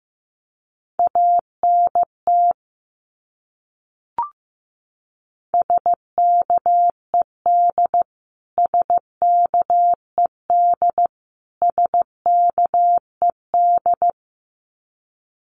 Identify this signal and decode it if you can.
Morse code, heard as a keyed tone, about 15 wpm, 700 Hz. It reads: ANT SKED SKED SKED
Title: Abbreviations Repeated 3x - 15wpm